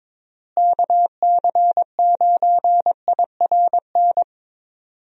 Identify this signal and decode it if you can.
Morse code, heard as a keyed tone, about 22 wpm, 700 Hz. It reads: KC9IRN
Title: USA Call Signs - 6 Characters - Repeated 3x - 22wpm